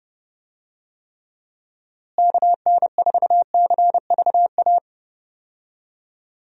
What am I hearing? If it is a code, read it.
Morse code, heard as a keyed tone, about 30 wpm, 700 Hz. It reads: KN4CVA
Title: USA Call Signs - 6 Characters - Repeated 2x - 30wpm